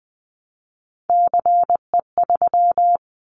Morse code, heard as a keyed tone, about 20 wpm, 700 Hz. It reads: CE3
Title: Callsign Prefixes for Top 100 DXCC - 20wpm